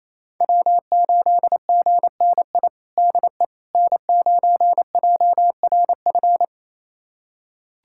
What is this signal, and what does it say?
Morse code, heard as a keyed tone, about 28 wpm, 700 Hz. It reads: W8GNI DE N9JRF
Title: USA Callsign DE Callsign - 5 Characters - 28wpm